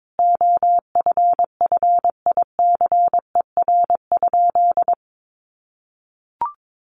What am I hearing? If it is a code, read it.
Morse code, heard as a keyed tone, about 22 wpm, 700 Hz. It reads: OFFICER?
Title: Sentences from Top 1200 Words 22wpm